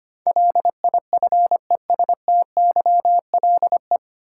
Morse code, heard as a keyed tone, about 25 wpm, 700 Hz. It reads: LIFESTYLE
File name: practice / 25wpm / 9 Letter Words - First 3 Letters Given - 25wpm